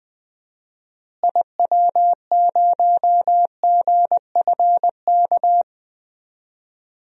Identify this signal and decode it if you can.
Morse code, heard as a keyed tone, about 21 wpm, 700 Hz. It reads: IW0GFK
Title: DXCC-100 Call Signs - 6 Characters - Repeated 2x - 20wpm